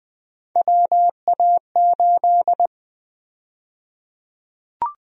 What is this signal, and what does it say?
Morse code, heard as a keyed tone, about 20 wpm, 700 Hz. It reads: WA8
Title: USA Call Sign Prefixes - 20wpm